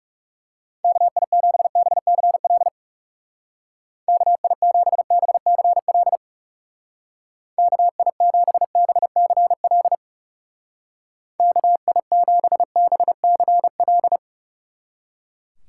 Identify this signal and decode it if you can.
Morse code, heard as a keyed tone, about 38 wpm, 700 Hz. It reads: KI7BCL KI7BCL KI7BCL KI7BCL